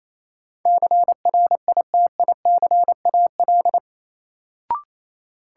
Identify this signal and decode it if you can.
Morse code, heard as a keyed tone, about 28 wpm, 700 Hz. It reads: CRITICAL
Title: Sentences from Top 1100 Words - New Word Drill - 28wpm